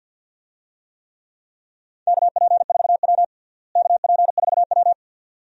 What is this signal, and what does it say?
Morse code, heard as a keyed tone, about 52 wpm, 700 Hz. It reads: KP4W KP4W
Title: DXCC-100 Call Signs - 4 Characters - Repeated 2x - 50wpm